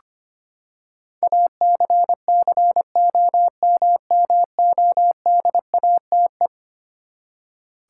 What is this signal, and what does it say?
Morse code, heard as a keyed tone, about 25 wpm, 700 Hz. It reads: ACCOMMODATE
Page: morse-code-ninja-practice-25wpm